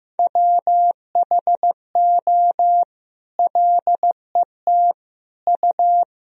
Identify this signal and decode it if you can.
Morse code, heard as a keyed tone, about 15 wpm, 700 Hz. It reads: WHO LET U
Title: Sentences from Top 300 Words and Abbreviations 15wpm